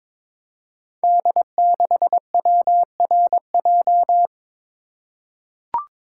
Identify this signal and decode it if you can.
Morse code, heard as a keyed tone, about 22 wpm, 700 Hz. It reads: D6WRJ
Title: Code Groups - 5 Letter:Numbers - 22wpm